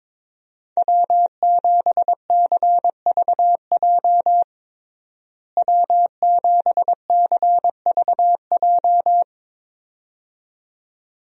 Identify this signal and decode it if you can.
Morse code, heard as a keyed tone, about 22 wpm, 700 Hz. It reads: W7CVJ W7CVJ
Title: USA Call Signs - 5 Characters - Repeated 2x - 22wpm